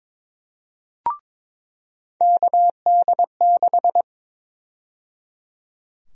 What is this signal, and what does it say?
Morse code, heard as a keyed tone, about 22 wpm, 700 Hz. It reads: KD6
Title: USA Call Sign Prefixes - 22wpm